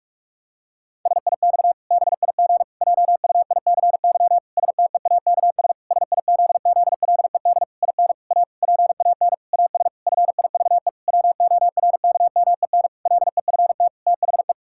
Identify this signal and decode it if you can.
Morse code, heard as a keyed tone, about 41 wpm, 700 Hz. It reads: SIX BIG JUICY STEAKS SIZZLED IN A PAN AS FIVE WORKMEN LEFT THE
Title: Pangram Challenge v2 - 45wpm